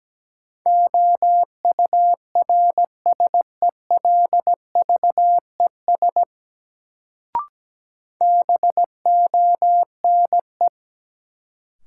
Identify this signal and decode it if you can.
Morse code, heard as a keyed tone, about 17 wpm, 700 Hz. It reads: OURSELVES BONE
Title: Sentences from Top 1600 Words - New Word Drill Alt - 17wpm